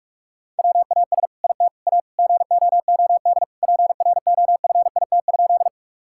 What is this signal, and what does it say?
Morse code, heard as a keyed tone, about 42 wpm, 700 Hz. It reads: WAS IT A GOOD PROFIT?
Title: Sentences from Top 1000 Words - Review Part 5 - 45wpm